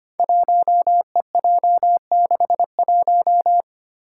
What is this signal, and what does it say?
Morse code, heard as a keyed tone, about 25 wpm, 700 Hz. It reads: 1EJ61
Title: Five Challenging Letter-Number Combinations 25wpm